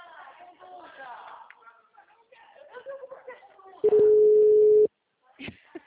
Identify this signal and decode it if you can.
Ringback tone (Europe, 425 Hz), with speech.
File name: out- -2010- - - .332